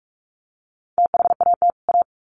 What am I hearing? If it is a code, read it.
Morse code, heard as a keyed tone, about 44 wpm, 700 Hz. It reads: THAT A
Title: Sets of 2 Words 45wpm